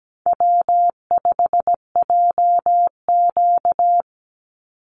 Morse code, heard as a keyed tone, about 17 wpm, 700 Hz. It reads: W5JQ
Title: Call Signs Repeated 3x - 17wpm